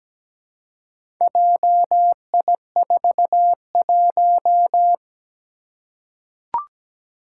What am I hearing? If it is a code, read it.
Morse code, heard as a keyed tone, about 17 wpm, 700 Hz. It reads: JI41
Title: Four Challenging Letter-Number Combinations 17wpm